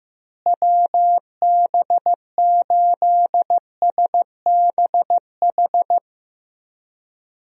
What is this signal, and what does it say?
Morse code, heard as a keyed tone, about 15 wpm, 700 Hz. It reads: WB8SBH
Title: USA Call Signs - 6 Characters - Repeated 3x - 15wpm